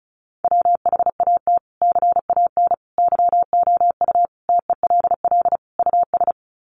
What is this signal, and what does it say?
Morse code, heard as a keyed tone, about 34 wpm, 700 Hz. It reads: WHAT CAN YOU TELL US
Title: Sets of 5 Words Encore 35wpm